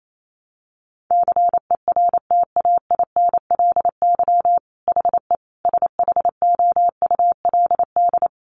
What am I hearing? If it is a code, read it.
Morse code, heard as a keyed tone, about 28 wpm, 690 Hz. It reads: CERTAINLY HE SHOULD
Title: Sentences from Top 900 Words 28wpm